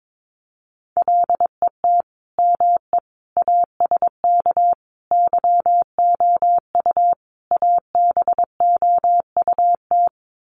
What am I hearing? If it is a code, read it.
Morse code, heard as a keyed tone, about 22 wpm, 700 Hz. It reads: LET ME ASK YOU ABOUT
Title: Sets of 5 Words 22wpm